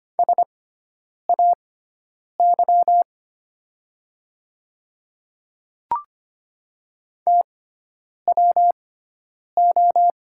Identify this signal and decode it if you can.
Morse code, heard as a keyed tone, about 25 wpm, spaced as for 9 wpm, 700 Hz. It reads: SAY TWO
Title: 6x Character Spacing - Top 200 Words - 25wpm